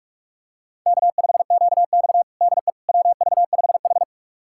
Morse code, heard as a keyed tone, about 46 wpm, 700 Hz. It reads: K5QX DE WV5H
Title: USA Callsign DE Callsign - 4 Characters - 45wpm